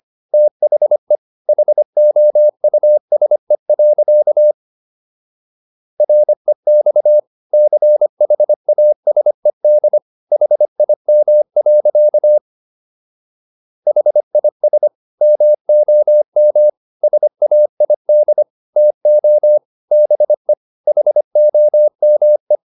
Morse code, heard as a keyed tone, about 25 wpm, 600 Hz. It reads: THE HOUSE. REX CHASED HIM. HIS MOM SAID TO BE HOME